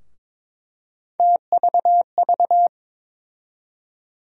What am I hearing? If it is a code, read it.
Morse code, heard as a keyed tone, about 22 wpm, 700 Hz. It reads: TVV
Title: USA Call Sign Postfixes - 3 Letters 22wpm